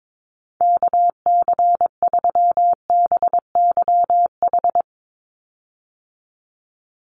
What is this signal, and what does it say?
Morse code, heard as a keyed tone, about 22 wpm, 700 Hz. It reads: KC3BYH
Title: USA Call Signs 22wpm